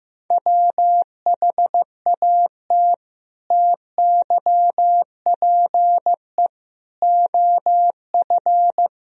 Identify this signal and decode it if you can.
Morse code, heard as a keyed tone, about 15 wpm, 700 Hz. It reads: WHAT TYPE OF
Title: Sentences from Top 1000 Words - Review Part 3 - 15wpm